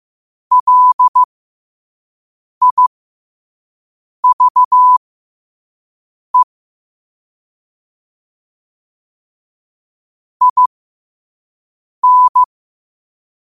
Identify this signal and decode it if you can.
Morse code, heard as a keyed tone, about 15 wpm, spaced as for 5 wpm, 1000 Hz. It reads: LIVE IN
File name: qotdfiles / 5wpm